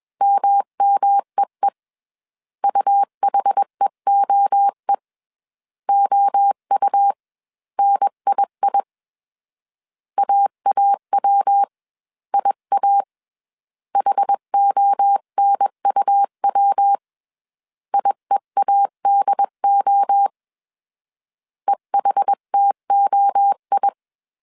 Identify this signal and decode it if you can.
Morse code, heard as a keyed tone, about 21 wpm, spaced as for 19 wpm, 800 Hz. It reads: MMEE UHEOE OUNII AAWIA HONUW IEADO EHTOI